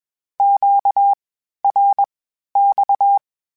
Morse code, heard as a keyed tone, about 21 wpm, spaced as for 12 wpm, 800 Hz. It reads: QRX
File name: (spaced) / Q-Calls Spaced-mp3s / Q Calls Twice sp @20